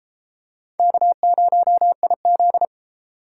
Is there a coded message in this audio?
Yes: Morse code from 0.8 to 2.6 s, heard as a keyed tone, about 33 wpm, 700 Hz. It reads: K0IZ